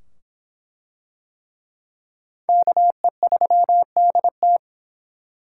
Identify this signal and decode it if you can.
Morse code, heard as a keyed tone, about 23 wpm, 700 Hz. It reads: KE3DT